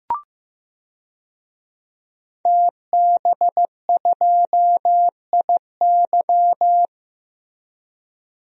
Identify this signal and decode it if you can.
Morse code, heard as a keyed tone, about 15 wpm, 700 Hz. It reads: TB2IY
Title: DXCC-100 Call Signs - 5 Characters - Repeated 2x - 15wpm